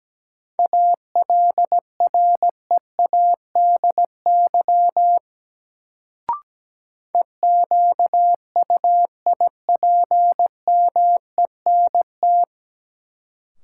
Morse code, heard as a keyed tone, about 17 wpm, 700 Hz. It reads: ALREADY EQUIPMENT
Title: Sentences from Top 700 Words - New Word Drill - 17wpm